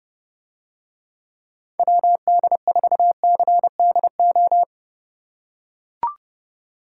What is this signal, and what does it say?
Morse code, heard as a keyed tone, about 30 wpm, 700 Hz. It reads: WD4CDO